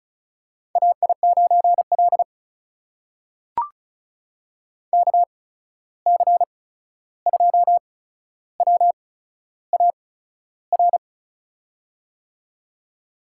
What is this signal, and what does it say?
Morse code, heard as a keyed tone, about 36 wpm, spaced as for 13 wpm, 700 Hz. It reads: AI9L KC2WAR
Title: USA Call Signs - 8x Character Spacing - 35wpm